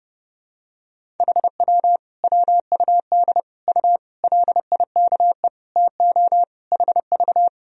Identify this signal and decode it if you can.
Morse code, heard as a keyed tone, about 30 wpm, 700 Hz. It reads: HW WUD U LIKE TO HV
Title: Sentences from Top 100 Words and Abbreviations 30wpm